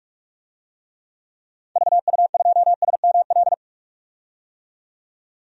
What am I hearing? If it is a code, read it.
Morse code, heard as a keyed tone, about 46 wpm, 700 Hz. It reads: VU2SML
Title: DXCC-100 Call Signs - 6 Characters - 45wpm